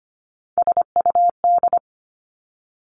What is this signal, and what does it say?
Morse code, heard as a keyed tone, about 25 wpm, 700 Hz. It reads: SUD
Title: Three Challenging Letter-Number Combinations 25wpm